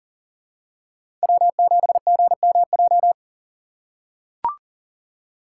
Morse code, heard as a keyed tone, about 40 wpm, 700 Hz. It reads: W7GMJ
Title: USA Call Signs - 5 Characters - 6x Character Spacing - 40wpm